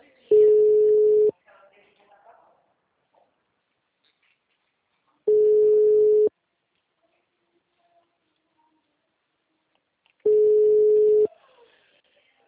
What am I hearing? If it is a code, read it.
Ringback tone (Europe, 425 Hz), with speech.